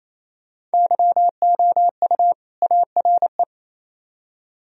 Morse code, heard as a keyed tone, about 38 wpm, 700 Hz. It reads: YOU ARE